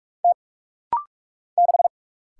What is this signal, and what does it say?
Morse code, heard as a keyed tone, about 45 wpm, 700 Hz. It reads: T 6
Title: Single Letter-Number - Rapid-Fire 45wpm